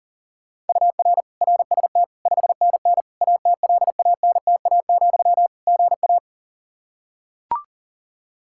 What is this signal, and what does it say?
Morse code, heard as a keyed tone, about 40 wpm, 700 Hz. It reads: UR RST 5NN ATLANTA, GA
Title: Practice QSO - Exchange 1 - Beginning - 40wpm